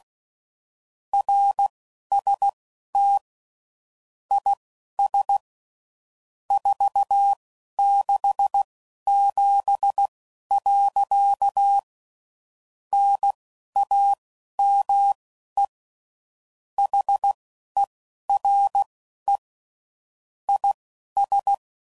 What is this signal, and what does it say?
Morse code, heard as a keyed tone, about 16 wpm, spaced as for 11 wpm, 800 Hz. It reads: RST IS 467. NAME HERE IS
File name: CW Code Groups @10WPM / QSOs / QSO kc7sac